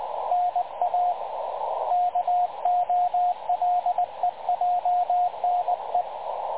Morse code, heard as a keyed tone, about 20 wpm, 720 Hz, with radio noise. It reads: NA KOLEJNE
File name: dlazaawansowanych / nagraniazpasma / websdr recording 3567.0kHz